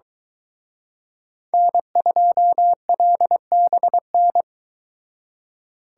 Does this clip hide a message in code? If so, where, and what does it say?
Yes: Morse code from 1.5 to 4.4 s, heard as a keyed tone, about 23 wpm, 700 Hz. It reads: N2LBN